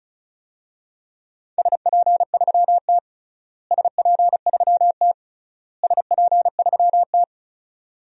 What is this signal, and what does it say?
Morse code, heard as a keyed tone, about 36 wpm, 700 Hz. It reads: SP3T SP3T SP3T
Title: DXCC-100 Call Signs - 4 Characters - Repeated 3x - 35wpm